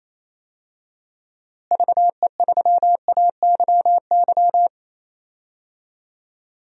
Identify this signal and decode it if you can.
Morse code, heard as a keyed tone, about 28 wpm, 700 Hz. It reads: VE3AYY
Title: DXCC-100 Call Signs - 6 Characters - Repeated 3x - 28wpm